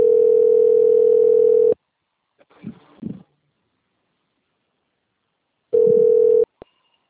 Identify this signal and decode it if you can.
Ringback tone (North America), 440 + 480 Hz.